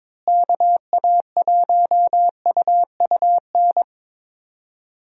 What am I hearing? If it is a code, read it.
Morse code, heard as a keyed tone, about 22 wpm, 700 Hz. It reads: KA1UUN
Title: USA Call Signs - 6 Characters - Repeated 2x - 22wpm